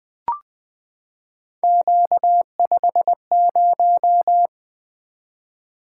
Morse code, heard as a keyed tone, about 20 wpm, 700 Hz. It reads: Q50
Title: Three Letter-Number Combinations - 20wpm